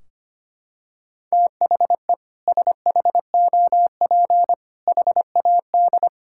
Morse code, heard as a keyed tone, about 25 wpm, 700 Hz. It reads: THE SHOP HAD
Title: Sentences from Top 1600 Words 25wpm